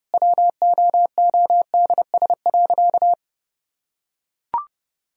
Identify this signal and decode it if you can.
Morse code, heard as a keyed tone, about 32 wpm, 700 Hz. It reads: WOODS.
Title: Sentences from Top 1000 Words - Review Part 2 - 30wpm